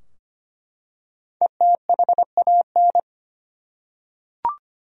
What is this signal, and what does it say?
Morse code, heard as a keyed tone, about 25 wpm, 700 Hz. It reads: ETHAN
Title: USA Common Names - Max 5 Letters - 25wpm